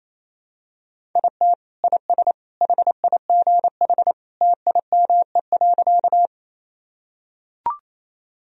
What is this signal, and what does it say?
Morse code, heard as a keyed tone, about 28 wpm, 700 Hz. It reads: IT IS HIGH TIME.
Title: Sentences from Top 1000 Words - Review Part 1 - 28wpm